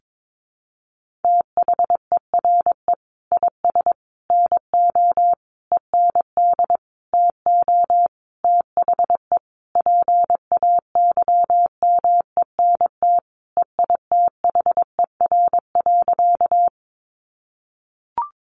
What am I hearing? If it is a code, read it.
Morse code, heard as a keyed tone, about 22 wpm, 700 Hz. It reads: THERE IS NO END TO THE PAYMENT EITHER.
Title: Sentences from Top 1200 Words 22wpm